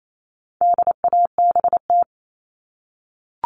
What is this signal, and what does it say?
Morse code, heard as a keyed tone, about 28 wpm, 700 Hz. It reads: DABT
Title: All Cut Numbers - 4-Digit 28wpm